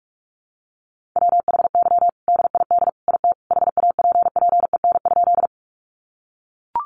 Morse code, heard as a keyed tone, about 44 wpm, 700 Hz. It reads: WHY DID IT HAPPEN?
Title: Sentences from Top 400 Words 45wpm